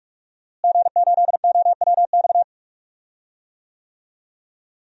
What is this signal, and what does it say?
Morse code, heard as a keyed tone, about 46 wpm, 700 Hz. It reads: G8OWX
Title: DXCC-100 Call Signs - 5 Characters - Repeated 3x - 45wpm